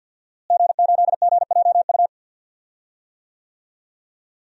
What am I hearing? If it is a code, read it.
Morse code, heard as a keyed tone, about 51 wpm, 700 Hz. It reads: G8GJU
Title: DXCC-100 Call Signs - 5 Characters - Repeated 2x - 50wpm